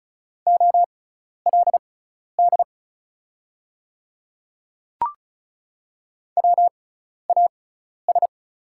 Morse code, heard as a keyed tone, about 36 wpm, spaced as for 12 wpm, 690 Hz. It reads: OLD WAS
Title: 6x Character Spacing - Top 300 Words - 35wpm